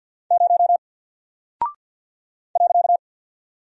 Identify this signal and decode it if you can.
Morse code, heard as a keyed tone, about 48 wpm, 700 Hz. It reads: 0 .
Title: Single Character (Letter:Number:Punctuation:Prosign) Alt 50wpm